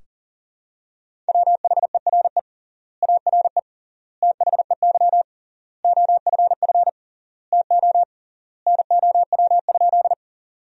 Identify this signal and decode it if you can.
Morse code, heard as a keyed tone, about 40 wpm, 700 Hz. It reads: WHERE ARE THEY OFF TO NOW?
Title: 3x Word-Spacing - Sentences from Top 200 Words - 40wpm